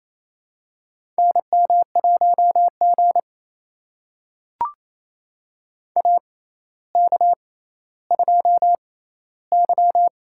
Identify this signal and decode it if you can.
Morse code, heard as a keyed tone, about 27 wpm, spaced as for 14 wpm, 700 Hz. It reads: NM1G AK2Y
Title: USA Call Signs - 4 Characters - 6x Character Spacing - 28wpm